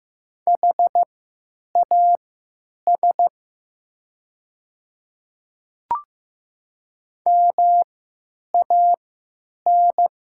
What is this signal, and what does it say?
Morse code, heard as a keyed tone, about 15 wpm, spaced as for 8 wpm, 700 Hz. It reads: HAS MAN
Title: 3x Character Spacing - Top 200 Words - 15wpm